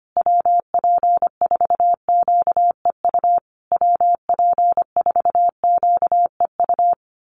Morse code, heard as a keyed tone, about 25 wpm, 700 Hz. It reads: WP4QEU WP4QEU
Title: USA Call Signs Repeated 3x 25wpm